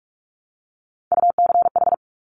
Morse code, heard as a keyed tone, about 46 wpm, 700 Hz. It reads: UCH
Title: Top Three Letter Combinations in Words 45wpm